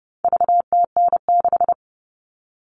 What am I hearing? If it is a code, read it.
Morse code, heard as a keyed tone, about 30 wpm, 700 Hz. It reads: VTN6
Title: All Cut Numbers - 4-Digit 30wpm